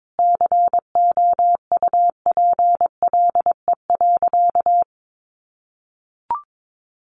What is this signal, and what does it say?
Morse code, heard as a keyed tone, about 22 wpm, 700 Hz. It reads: COUPLE.
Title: Sentences from Top 1200 Words 22wpm